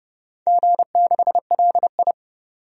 Morse code, heard as a keyed tone, about 30 wpm, 700 Hz. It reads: G6LI
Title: DXCC-100 Call Signs - 4 Characters - 30wpm